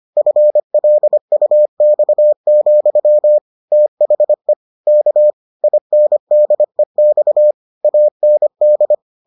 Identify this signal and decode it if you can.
Morse code, heard as a keyed tone, about 25 wpm, 600 Hz. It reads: FLUX, THE K INDEX AND